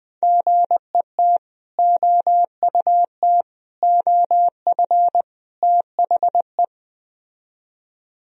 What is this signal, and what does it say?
Morse code, heard as a keyed tone, about 20 wpm, 700 Hz. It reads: GET OUT OF THE
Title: Speed-Racing - Sets of 4 Words 17wpm